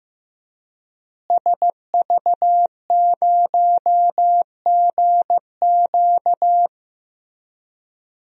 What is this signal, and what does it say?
Morse code, heard as a keyed tone, about 15 wpm, 700 Hz. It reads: SV0GQ
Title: DXCC-100 Call Signs - 5 Characters - Repeated 3x - 15wpm